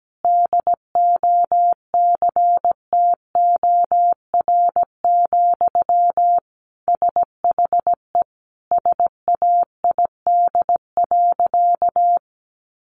Morse code, heard as a keyed tone, about 17 wpm, 700 Hz. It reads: DOCTOR, SHE SAID.
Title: Sentences from Top 1500 Words 17wpm